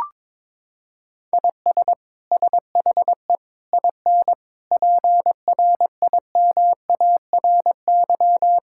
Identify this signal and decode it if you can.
Morse code, heard as a keyed tone, about 21 wpm, 700 Hz. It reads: IS SHE IN PRIMARY
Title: Sentences from Top 1000 Words - Review Part 5 - 22wpm